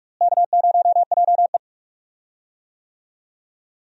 Morse code, heard as a keyed tone, about 45 wpm, 700 Hz. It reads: K0JE
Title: USA Call Signs - 4 Characters - Repeated 2x - 45wpm